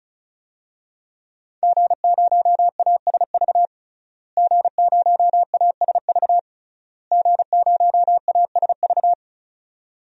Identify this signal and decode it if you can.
Morse code, heard as a keyed tone, about 34 wpm, 700 Hz. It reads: G0ASV G0ASV G0ASV
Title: DXCC-100 Call Signs - 5 Characters - Repeated 3x - 35wpm